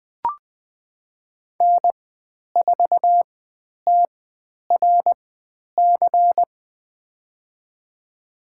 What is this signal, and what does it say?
Morse code, heard as a keyed tone, about 20 wpm, spaced as for 10 wpm, 700 Hz. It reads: N4TRC
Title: 102 Lesson 17g - Letter u - Call Signs 20wpm